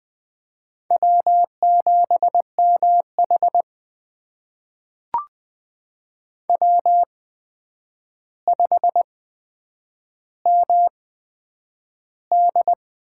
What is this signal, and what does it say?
Morse code, heard as a keyed tone, about 20 wpm, spaced as for 9 wpm, 700 Hz. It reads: W7MH W5MD
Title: USA Call Signs - 4 Characters - 8x Character Spacing - 20wpm